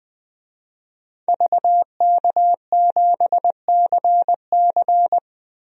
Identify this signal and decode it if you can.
Morse code, heard as a keyed tone, about 20 wpm, 700 Hz. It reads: VK7CC
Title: DXCC-100 Call Signs - 5 Characters - 20wpm